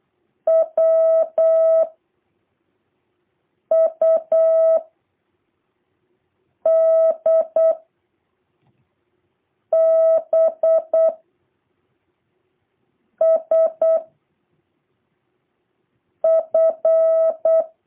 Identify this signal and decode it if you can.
Morse code, heard as a keyed tone, about 8 wpm, 650 Hz. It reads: W U D B S F
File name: jh8cbh / 2023cw